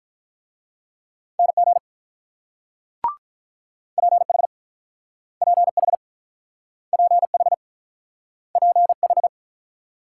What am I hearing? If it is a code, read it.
Morse code, heard as a keyed tone, about 47 wpm, 700 Hz. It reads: NG PH PH PH PH